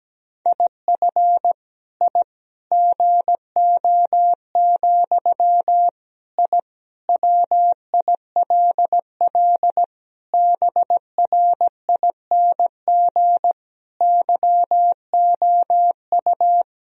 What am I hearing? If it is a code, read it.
Morse code, heard as a keyed tone, about 17 wpm, 700 Hz. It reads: IF I GO, I WILL BRING YOU
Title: Sentences from Top 1000 Words - Review Part 3 - 17wpm